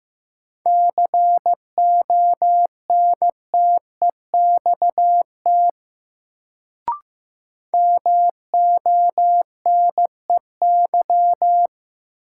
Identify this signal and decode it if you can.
Morse code, heard as a keyed tone, about 15 wpm, 700 Hz. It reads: CONTEXT MONEY
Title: Sets of 4 Words - New Word Drill - 15wpm